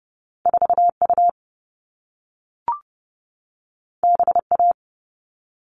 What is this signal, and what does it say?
Morse code, heard as a keyed tone, about 30 wpm, 700 Hz. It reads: 4U BA
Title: All Cut Numbers - 2-Digit 30wpm